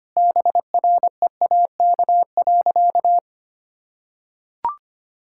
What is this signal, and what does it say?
Morse code, heard as a keyed tone, about 25 wpm, 700 Hz. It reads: BREAK.
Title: Sentences from Top 1000 Words - Review Part 6 - 25wpm